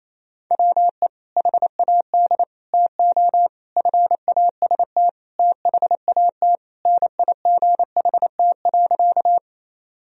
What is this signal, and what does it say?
Morse code, heard as a keyed tone, about 28 wpm, 700 Hz. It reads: WE HAD TO FAST THAT NIGHT.
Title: Sentences from Top 1000 Words - Review Part 2 - 28wpm